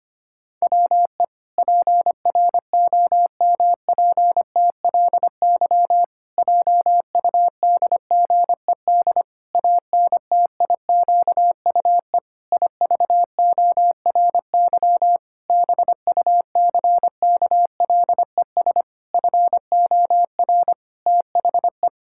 Morse code, heard as a keyed tone, about 25 wpm, 700 Hz. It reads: WE PROMPTLY JUDGED ANTIQUE IVORY BUCKLES FOR THE
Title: Pangram Challenge v2 - 25wpm